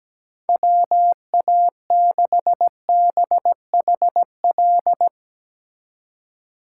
Morse code, heard as a keyed tone, about 17 wpm, 700 Hz. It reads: WA6BHL
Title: Speed Racing - USA 2x3 Call Signs 17wpm